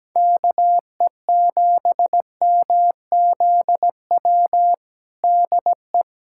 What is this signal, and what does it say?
Morse code, heard as a keyed tone, about 17 wpm, 700 Hz. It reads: KE7MZW DE
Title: USA Callsign DE Callsign - 6 Characters - 17wpm